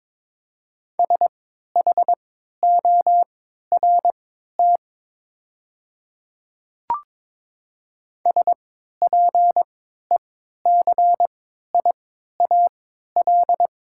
Morse code, heard as a keyed tone, about 22 wpm, spaced as for 13 wpm, 700 Hz. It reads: SHORT SPECIAL
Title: 3x Character Spacing - Top 500 Words - 22wpm